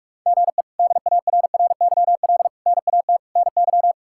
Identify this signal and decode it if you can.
Morse code, heard as a keyed tone, about 46 wpm, 700 Hz. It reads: GE DARRYL NAT NY
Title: K1USN SST QSO Practice - 45wpm